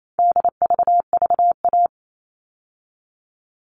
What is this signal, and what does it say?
Morse code, heard as a keyed tone, about 28 wpm, 700 Hz. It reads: DVVA
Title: All Cut Numbers - 4-Digit 28wpm